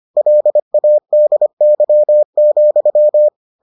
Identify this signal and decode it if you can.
Morse code, heard as a keyed tone, about 25 wpm, 600 Hz. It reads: LADY,